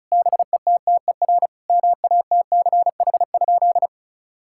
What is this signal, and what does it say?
Morse code, heard as a keyed tone, about 36 wpm, 700 Hz. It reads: BETTER MATCH?
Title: Sentences from Top 1600 Words 35wpm